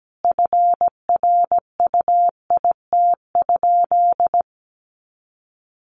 Morse code, heard as a keyed tone, about 17 wpm, 700 Hz. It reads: FRUIT?
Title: Sentences from Top 1500 Words 17wpm